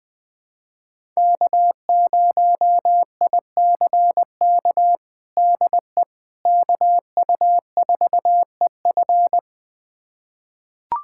Morse code, heard as a keyed tone, about 20 wpm, 700 Hz. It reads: K0ICK DE KU4EF
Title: USA Callsign DE Callsign - 5 Characters - 20wpm